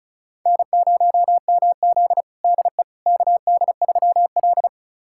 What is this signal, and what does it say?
Morse code, heard as a keyed tone, about 36 wpm, 700 Hz. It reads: N0MZ DE KD3L
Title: USA Callsign DE Callsign - 4 Characters - 35wpm